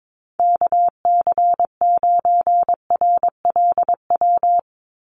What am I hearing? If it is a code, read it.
Morse code, heard as a keyed tone, about 22 wpm, 700 Hz. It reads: KC9RLW
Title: USA Call Signs 22wpm